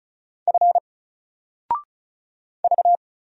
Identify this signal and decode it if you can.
Morse code, heard as a keyed tone, about 36 wpm, 700 Hz. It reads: F V
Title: Single Letter-Number 35wpm